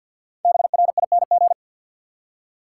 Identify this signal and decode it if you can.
Morse code, heard as a keyed tone, about 50 wpm, 700 Hz. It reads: BRING
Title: Five Letter Words - 50wpm